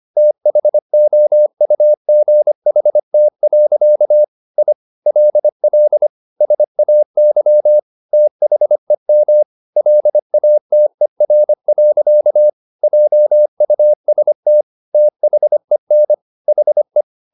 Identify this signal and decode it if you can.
Morse code, heard as a keyed tone, about 25 wpm, 600 Hz. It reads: THOUGHT. I LL SAY THEM LATER. JUST THEN HE